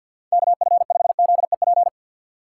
Morse code, heard as a keyed tone, about 52 wpm, 700 Hz. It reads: KF5ZEP
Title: USA Call Signs - 6 Characters - 6x Character Spacing - 50wpm